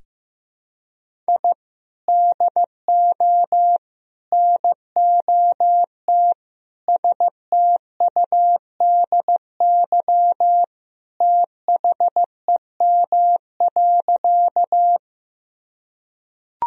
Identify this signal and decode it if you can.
Morse code, heard as a keyed tone, about 15 wpm, 700 Hz. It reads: I DO NOT STUDY THEM.
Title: Sentences from Top 1000 Words - Review Part 1 - 15wpm